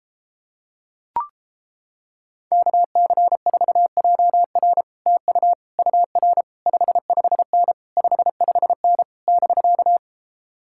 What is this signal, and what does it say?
Morse code, heard as a keyed tone, about 29 wpm, 700 Hz. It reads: KC4JR TU UR 55N 55N <BK>